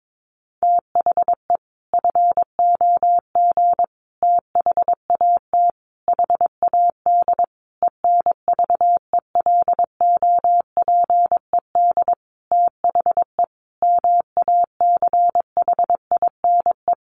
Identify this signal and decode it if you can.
Morse code, heard as a keyed tone, about 22 wpm, 700 Hz. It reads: THE FOG THAT HAD ENVELOPED THE MACHINE
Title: A Sound Of Thunder 22wpm